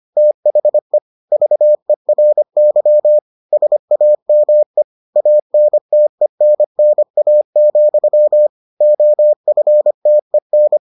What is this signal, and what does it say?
Morse code, heard as a keyed tone, about 25 wpm, 600 Hz. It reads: THE VERY SAME ANTENNA, OFTEN